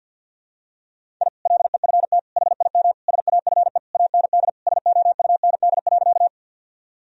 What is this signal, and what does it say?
Morse code, heard as a keyed tone, about 49 wpm, 700 Hz. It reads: I LEFT HIM SAFE AND SOUND.
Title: Sentences from Top 1000 Words - Review Part 1 - 50wpm